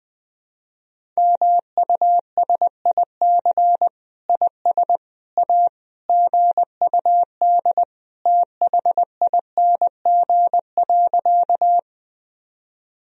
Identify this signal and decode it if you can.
Morse code, heard as a keyed tone, about 21 wpm, 700 Hz. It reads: MUSIC IS A GUD THING.
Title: Sentences from Top 300 Words and Abbreviations 20wpm